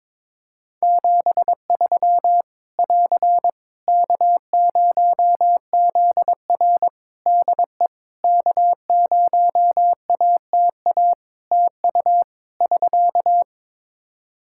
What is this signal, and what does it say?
Morse code, heard as a keyed tone, about 22 wpm, 700 Hz. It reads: 73 <AR> K0ZR DE K0ATA TU <SK>
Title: Practice QSO - Exchange 4 - Ending - 22wpm